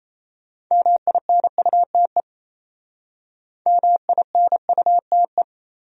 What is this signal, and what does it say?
Morse code, heard as a keyed tone, about 33 wpm, 700 Hz. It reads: MINUTE MINUTE
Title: Speed-Racing Top 500 Words 28wpm